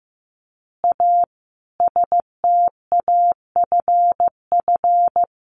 Morse code, heard as a keyed tone, about 15 wpm, 700 Hz. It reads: A STAFF
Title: Additional Easy Phrases 15wpm